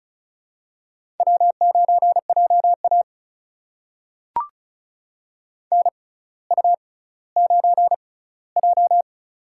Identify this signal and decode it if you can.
Morse code, heard as a keyed tone, about 36 wpm, spaced as for 18 wpm, 700 Hz. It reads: W9JA NU9J
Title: USA Call Signs - 4 Characters - 6x Character Spacing - 35wpm